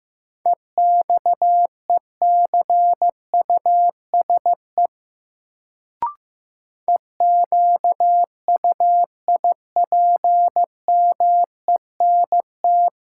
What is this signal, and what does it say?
Morse code, heard as a keyed tone, about 15 wpm, 700 Hz. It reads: EXECUSE EQUIPMENT
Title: Sentences from Top 700 Words - New Word Drill - 15wpm